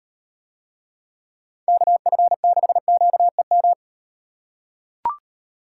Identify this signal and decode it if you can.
Morse code, heard as a keyed tone, about 38 wpm, 700 Hz. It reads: KF6QEM